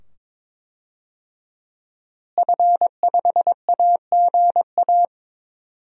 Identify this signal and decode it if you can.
Morse code, heard as a keyed tone, about 21 wpm, 700 Hz. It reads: F5AGA
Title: DXCC-100 Call Signs - 5 Characters - Repeated 2x - 22wpm